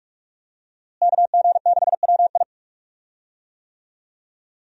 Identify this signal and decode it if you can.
Morse code, heard as a keyed tone, about 46 wpm, 700 Hz. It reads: KG6WI
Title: USA Call Signs - 5 Characters - Repeated 3x - 45wpm